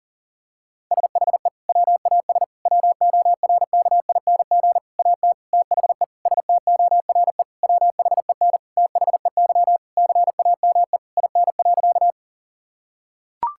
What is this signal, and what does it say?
Morse code, heard as a keyed tone, about 40 wpm, 700 Hz. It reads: SHE WAS WORKING AT THE STORE WHEN THEY CAME IN.
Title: Sentences from Top 1000 Words - Review Part 5 - 40wpm